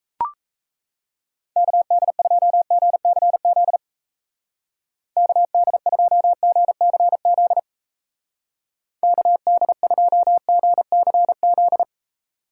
Morse code, heard as a keyed tone, about 37 wpm, 700 Hz. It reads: KD2GCZ KD2GCZ KD2GCZ